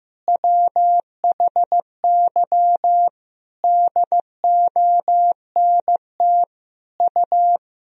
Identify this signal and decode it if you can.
Morse code, heard as a keyed tone, about 15 wpm, 700 Hz. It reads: WHY DONT U
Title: Sentences from Top 300 Words and Abbreviations 15wpm